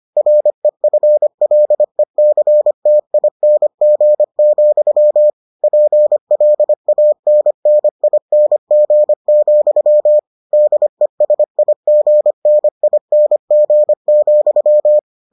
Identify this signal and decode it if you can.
Morse code, heard as a keyed tone, about 25 wpm, 600 Hz. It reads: REFLECTING, PLANNING, DESIGNING,